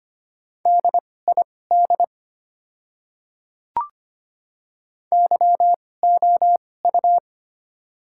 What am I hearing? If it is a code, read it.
Morse code, heard as a keyed tone, about 25 wpm, spaced as for 18 wpm, 700 Hz. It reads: DID YOU
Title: 2x Character Spacing - Top 100 Words - 25wpm